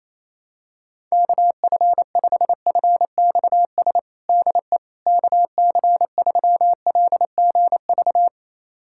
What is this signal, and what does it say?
Morse code, heard as a keyed tone, about 28 wpm, 700 Hz. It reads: KF5FXS DE KC3LGV
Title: USA Callsign DE Callsign - 6 Characters - 28wpm